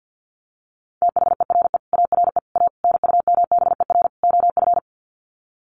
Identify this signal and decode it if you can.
Morse code, heard as a keyed tone, about 50 wpm, 700 Hz. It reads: THERE ARE A NUMBER OF
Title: Sets of 5 Words 50wpm